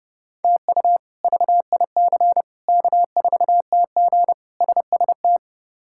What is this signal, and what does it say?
Morse code, heard as a keyed tone, about 30 wpm, 700 Hz. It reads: TU VIC K4TG SST
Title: K1USN SST QSO Practice - 30wpm